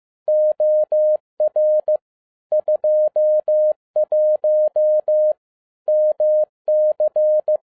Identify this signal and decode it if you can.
Morse code, heard as a keyed tone, about 15 wpm, 610 Hz. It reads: OR 21 MC